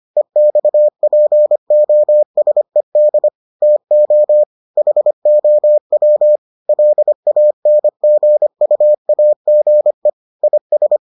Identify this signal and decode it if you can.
Morse code, heard as a keyed tone, about 25 wpm, 600 Hz. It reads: EXPOSED TO HOW LANGUAGE IS